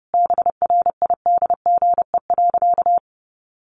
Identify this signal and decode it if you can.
Morse code, heard as a keyed tone, about 30 wpm, 700 Hz. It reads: BRIDGE.
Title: Sentences from Top 1200 Words 30wpm